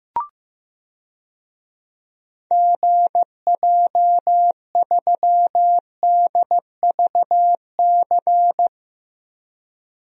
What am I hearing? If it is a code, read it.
Morse code, heard as a keyed tone, about 15 wpm, 700 Hz. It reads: GJ3DVC
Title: DXCC-100 Call Signs - 6 Characters - Repeated 2x - 15wpm